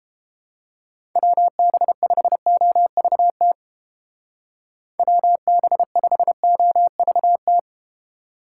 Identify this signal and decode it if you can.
Morse code, heard as a keyed tone, about 30 wpm, 700 Hz. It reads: WB5OVT WB5OVT